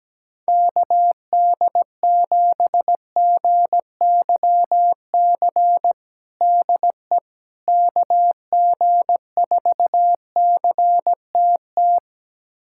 Morse code, heard as a keyed tone, about 17 wpm, 700 Hz. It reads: KD7GYC DE KG4CTT
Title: USA Callsign DE Callsign - 6 Characters - 17wpm